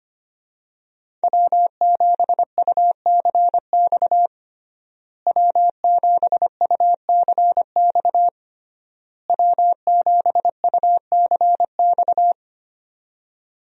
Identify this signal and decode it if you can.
Morse code, heard as a keyed tone, about 25 wpm, 700 Hz. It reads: W7UCX W7UCX W7UCX
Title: USA Call Signs - 5 Characters - Repeated 3x - 25wpm